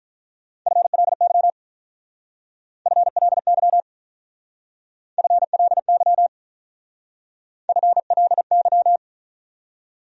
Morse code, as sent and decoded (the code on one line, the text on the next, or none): ..-. .-.. -.-- / ..-. .-.. -.-- / ..-. .-.. -.-- / ..-. .-.. -.--
FLY FLY FLY FLY